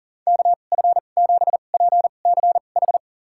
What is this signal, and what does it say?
Morse code, heard as a keyed tone, about 40 wpm, spaced as for 29 wpm, 700 Hz. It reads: KF7PCH
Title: USA Call Signs - 2x Character Spacing - 40wpm